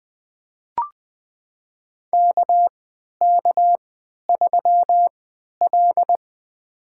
Morse code, heard as a keyed tone, about 20 wpm, spaced as for 11 wpm, 700 Hz. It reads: KK3L
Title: USA Call Signs - 4 Characters - 3x Character Spacing - 20wpm